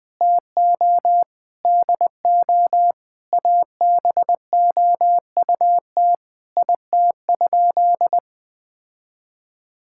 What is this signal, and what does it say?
Morse code, heard as a keyed tone, about 20 wpm, 700 Hz. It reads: TO DO ABOUT IT?
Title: Sentences with 5-Word Phrases - 20wpm